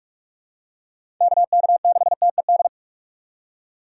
Morse code, heard as a keyed tone, about 46 wpm, 700 Hz. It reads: KK6TED
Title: USA Call Signs - 6 Characters - 4x Character Spacing - 45wpm